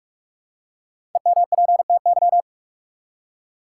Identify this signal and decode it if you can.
Morse code, heard as a keyed tone, about 45 wpm, 700 Hz. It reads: EMPTY
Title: Five Letter Words - 45wpm